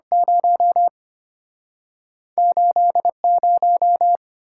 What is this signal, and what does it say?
Morse code, heard as a keyed tone, about 34 wpm, spaced as for 30 wpm, 700 Hz. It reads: 0 80
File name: practice / 25wpm / Speed-Racing - Two Letter-Number Combinations - 25wpm